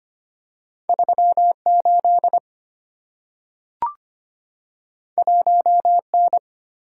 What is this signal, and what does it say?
Morse code, heard as a keyed tone, about 25 wpm, 700 Hz. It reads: 38 1N
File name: Two Letter-Number Combinations - 25wpm